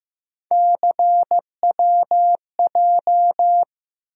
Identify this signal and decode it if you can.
Morse code, heard as a keyed tone, about 15 wpm, 700 Hz. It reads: CWJ
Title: USA Call Sign Postfixes - 3 Letters 15wpm